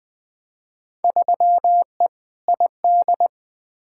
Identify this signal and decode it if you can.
Morse code, heard as a keyed tone, about 20 wpm, 700 Hz. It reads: 3E ID
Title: Field Day Exchanges Encore - 20wpm